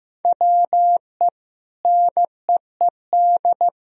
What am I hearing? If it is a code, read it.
Morse code, heard as a keyed tone, about 16 wpm, 700 Hz. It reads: WE NEED
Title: Sentences from Top 1000 Words - Review Part 4 - 15wpm